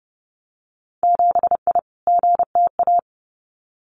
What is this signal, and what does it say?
Morse code, heard as a keyed tone, about 30 wpm, 700 Hz. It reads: 7I GTA
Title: Winter Field Day Exchanges Encore 30wpm